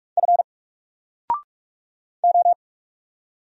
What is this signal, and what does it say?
Morse code, heard as a keyed tone, about 44 wpm, 700 Hz. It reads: F O
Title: Single Character (Letter:Number:Punctuation:Prosign) 45wpm